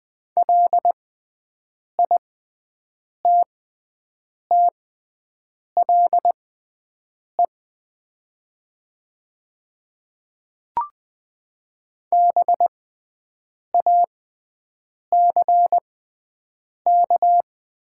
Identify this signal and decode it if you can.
Morse code, heard as a keyed tone, about 20 wpm, spaced as for 7 wpm, 700 Hz. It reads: LITTLE BACK
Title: 6x Character Spacing - Top 300 Words - 20wpm